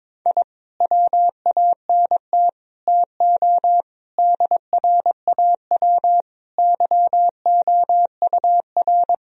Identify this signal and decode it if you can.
Morse code, heard as a keyed tone, about 22 wpm, 700 Hz. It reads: I WANT TO DRAW YOUR
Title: Sentences from Top 1000 Words - Review Part 1 - 22wpm